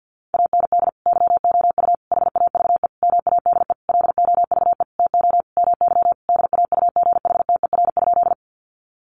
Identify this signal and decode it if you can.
Morse code, heard as a keyed tone, about 50 wpm, 700 Hz. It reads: AND YOU HAVE MADE LOVE TO MY DAUGHTER?
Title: Sentences from Top 1500 Words 50wpm